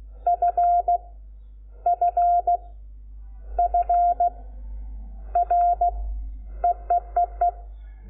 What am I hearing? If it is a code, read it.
Morse code, heard as a keyed tone, about 16 wpm, 700 Hz. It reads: F F F R EEEE
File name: Lecciones / CLASE Nº5